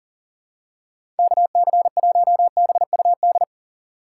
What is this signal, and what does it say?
Morse code, heard as a keyed tone, about 37 wpm, 700 Hz. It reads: KC1BUD
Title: USA Call Signs - 4x Character Spacing - 40wpm